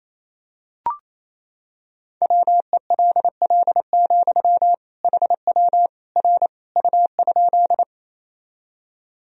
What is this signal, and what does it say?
Morse code, heard as a keyed tone, about 27 wpm, 700 Hz. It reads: WELL, HW R U?
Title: Sentences from Top 200 Words and Abbreviations 28wpm